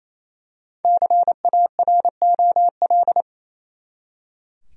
Morse code, heard as a keyed tone, about 28 wpm, 700 Hz. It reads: CAROL
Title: USA Common Names - 28wpm